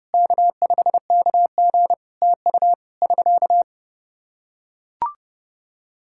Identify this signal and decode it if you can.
Morse code, heard as a keyed tone, about 30 wpm, 700 Hz. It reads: K5KG TU <SK>
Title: Practice QSO - Exchange 4 - Ending - 30wpm